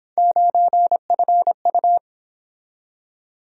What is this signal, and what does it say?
Morse code, heard as a keyed tone, about 27 wpm, 700 Hz. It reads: 9FU
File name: Speed-Racing - Three Letter-Number Combinations - 22wpm